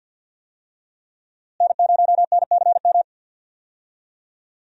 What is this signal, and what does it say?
Morse code, heard as a keyed tone, about 47 wpm, 700 Hz. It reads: N0NCM
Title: USA Call Signs - 5 Characters - Repeated 3x - 50wpm